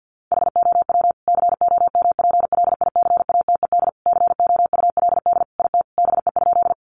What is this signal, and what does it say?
Morse code, heard as a keyed tone, about 50 wpm, 700 Hz. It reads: HOW COMPLICATED COULD IT BE?
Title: Sentences from Top 1200 Words 50wpm